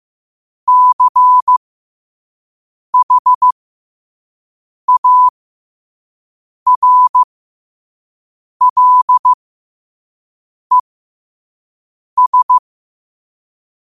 Morse code, heard as a keyed tone, about 15 wpm, spaced as for 5 wpm, 1000 Hz. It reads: CHARLES
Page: learn-to-code